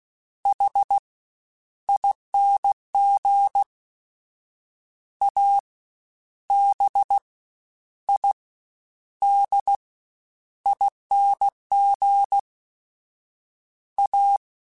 Morse code, heard as a keyed tone, about 16 wpm, 800 Hz. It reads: H ING A B I D ING A
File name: Special INGs-mp3s / Special INGs-mp3s / SlowToFaster ing @15